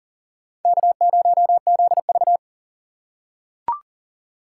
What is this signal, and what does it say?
Morse code, heard as a keyed tone, about 40 wpm, 700 Hz. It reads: K0ZV
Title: USA Call Signs - 4 Characters - 4x Character Spacing - 40wpm